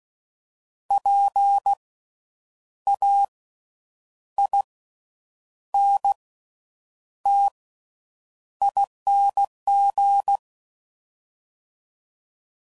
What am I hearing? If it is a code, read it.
Morse code, heard as a keyed tone, about 16 wpm, spaced as for 8 wpm, 800 Hz. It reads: PAINTING
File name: Suffixes / Special INGs / Slow With Faster Ings